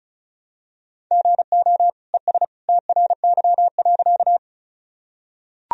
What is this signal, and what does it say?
Morse code, heard as a keyed tone, about 35 wpm, 700 Hz. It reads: GO ES TRY.
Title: Sentences from Top 200 Words and Abbreviations 35wpm